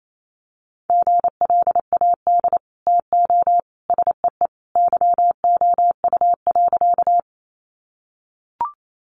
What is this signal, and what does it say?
Morse code, heard as a keyed tone, about 28 wpm, 700 Hz. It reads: GLAD TO SEE YOU.
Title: Sentences from Top 800 Words 28wpm